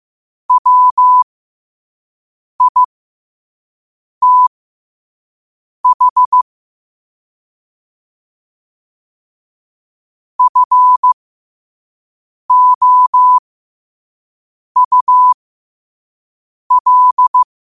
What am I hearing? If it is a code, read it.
Morse code, heard as a keyed tone, about 15 wpm, spaced as for 5 wpm, 1000 Hz. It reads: WITH FOUL
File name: qotdfiles / 5wpm